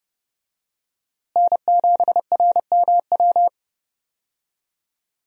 Morse code, heard as a keyed tone, about 30 wpm, 700 Hz. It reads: N7RMW
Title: USA Call Signs - 5 Characters - Repeated 3x - 30wpm